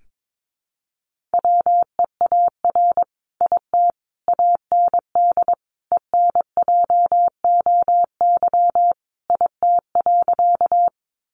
Morse code, heard as a keyed tone, about 22 wpm, 700 Hz. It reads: WEAR IT AND ENJOY IT.
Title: Sentences from Top 1500 Words 22wpm